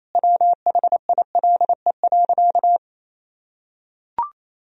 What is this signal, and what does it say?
Morse code, heard as a keyed tone, about 28 wpm, 700 Hz. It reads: WHILE.
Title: Sentences from Top 1000 Words - Review Part 1 - 28wpm